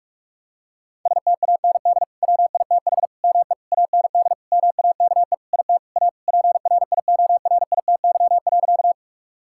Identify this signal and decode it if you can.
Morse code, heard as a keyed tone, about 46 wpm, 700 Hz. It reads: STAND WITH ME AND MAKE IT A PRIORITY.
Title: Sentences from Top 1600 Words 45wpm